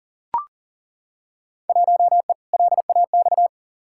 Morse code, heard as a keyed tone, about 40 wpm, 700 Hz. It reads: 1E LAX
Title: Field Day Exchanges - 40wpm